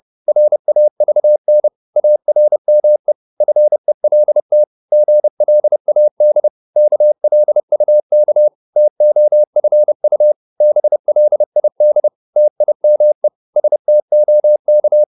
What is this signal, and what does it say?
Morse code, heard as a keyed tone, about 30 wpm, 600 Hz. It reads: RAVN ARME FELT GLAD KLUK TOFU BLID TIME STOK
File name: Korte ord DK 30wpm